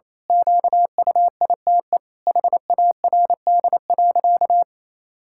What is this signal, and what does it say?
Morse code, heard as a keyed tone, about 27 wpm, 700 Hz. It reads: QUITE HARD.
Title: Pangram Challenge v2 - 28wpm